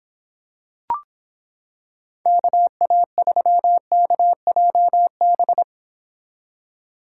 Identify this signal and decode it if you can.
Morse code, heard as a keyed tone, about 26 wpm, 700 Hz. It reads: KA3KJB